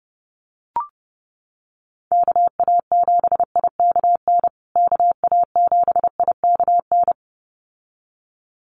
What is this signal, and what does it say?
Morse code, heard as a keyed tone, about 30 wpm, 700 Hz. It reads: KA7IKN KA7IKN
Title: USA Call Signs Repeated 2x 30wpm